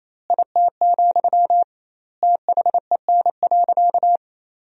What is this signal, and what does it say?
Morse code, heard as a keyed tone, about 28 wpm, 700 Hz. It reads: IT, THEN.
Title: 2x Word-Spacing - Sentences from Top 200 Words - 28wpm